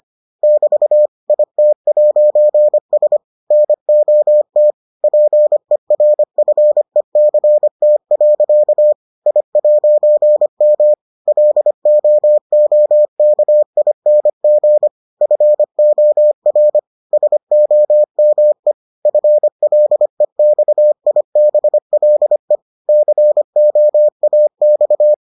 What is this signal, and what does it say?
Morse code, heard as a keyed tone, about 25 wpm, 600 Hz. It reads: <BT> IT'S NOT PERFECT. I'M LOOKING FOR SOME FLEXIBLE COAX